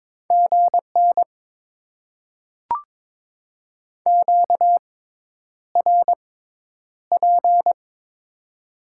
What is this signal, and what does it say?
Morse code, heard as a keyed tone, about 22 wpm, spaced as for 10 wpm, 700 Hz. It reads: GN QRP
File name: Top 90 QSO Elements - 6x Character Spacing - 22wpm